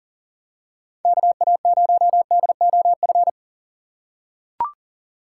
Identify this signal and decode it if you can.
Morse code, heard as a keyed tone, about 40 wpm, 700 Hz. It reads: KA0DOF
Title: USA Call Signs - 6 Characters - 3x Character Spacing - 40wpm